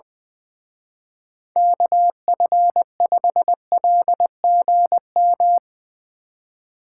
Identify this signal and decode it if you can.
Morse code, heard as a keyed tone, about 20 wpm, 700 Hz. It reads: KF5LGM
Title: Speed Racing - USA 2x3 Call Signs 20wpm